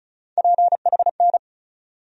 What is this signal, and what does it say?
Morse code, heard as a keyed tone, about 36 wpm, 700 Hz. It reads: PHN
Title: USA Call Sign Postfixes - 3 Letters 35wpm